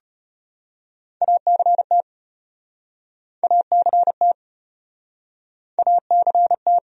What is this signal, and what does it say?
Morse code, heard as a keyed tone, about 35 wpm, 700 Hz. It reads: ACT ACT ACT